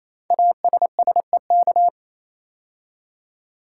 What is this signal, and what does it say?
Morse code, heard as a keyed tone, about 27 wpm, 700 Hz. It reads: ASSEK
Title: Code Groups - 5 Letters - 28wpm